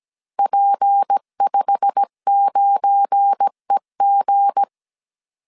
Morse code, heard as a keyed tone, about 17 wpm, 800 Hz. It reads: P59EG
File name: CW / DET / 1 9 1500C S C 16wpm 2909